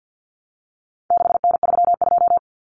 Morse code, heard as a keyed tone, about 48 wpm, 700 Hz. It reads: 6N32
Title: Common Cut Numbers - 4-Digit 50wpm